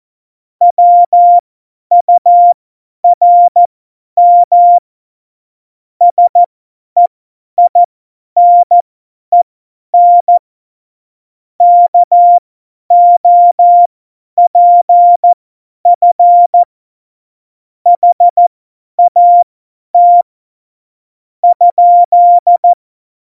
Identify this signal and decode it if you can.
Morse code, heard as a keyed tone, about 14 wpm, spaced as for 9 wpm, 700 Hz. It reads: WURM SEINEN KOPF HAT?
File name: Morsekurs / Witze / Witz 02. Oktober 2018, 50 F70